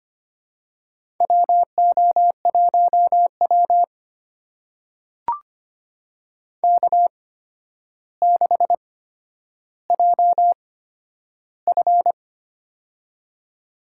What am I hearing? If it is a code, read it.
Morse code, heard as a keyed tone, about 25 wpm, spaced as for 11 wpm, 700 Hz. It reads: WO1W K6JF
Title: USA Call Signs - 4 Characters - 8x Character Spacing - 25wpm